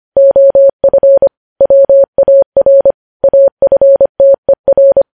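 Morse code, heard as a keyed tone, about 25 wpm, 550 Hz. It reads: OF WAR AFTER
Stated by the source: Morse code news headlines at 25 words per minute.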